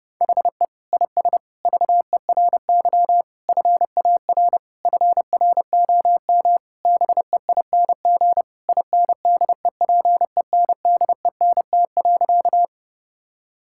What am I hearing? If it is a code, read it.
Morse code, heard as a keyed tone, about 31 wpm, 700 Hz. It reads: HE IS VERY FAR FROM BEING INDEPENDENT.
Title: Sentences from Top 1000 Words - Review Part 6 - 30wpm